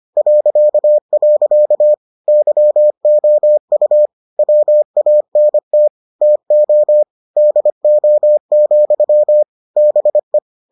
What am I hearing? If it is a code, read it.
Morse code, heard as a keyed tone, about 25 wpm, 600 Hz. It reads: .. YOU WANT TO DO, BE